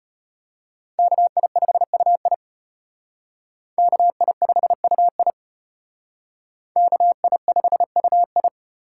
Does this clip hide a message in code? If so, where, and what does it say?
Yes: Morse code from 1.0 to 8.5 s, heard as a keyed tone, about 32 wpm, 700 Hz. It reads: KI5UI KI5UI KI5UI